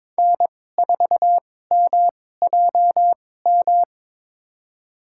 Morse code, heard as a keyed tone, about 22 wpm, spaced as for 16 wpm, 700 Hz. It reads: N4MJM
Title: USA Call Signs - 5 Characters - 2x Character Spacing - 22wpm